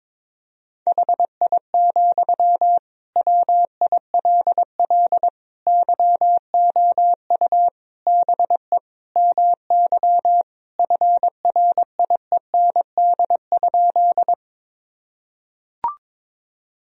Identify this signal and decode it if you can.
Morse code, heard as a keyed tone, about 22 wpm, 700 Hz. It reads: HI, WILL YOU BE MY FRIEND?
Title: Sentences from Top 1000 Words - Review Part 3 - 22wpm